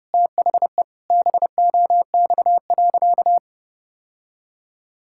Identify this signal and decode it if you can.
Morse code, heard as a keyed tone, about 30 wpm, 700 Hz. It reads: THE BOX.
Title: Sentences from Top 1000 Words - Review Part 5 - 30wpm